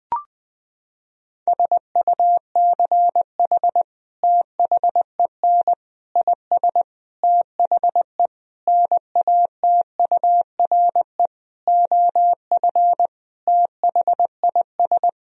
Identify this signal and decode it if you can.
Morse code, heard as a keyed tone, about 20 wpm, 700 Hz. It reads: SUCH THEN IS THE NATURE OF THIS
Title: Sentences from Top 1000 Words - Review Part 4 - 20wpm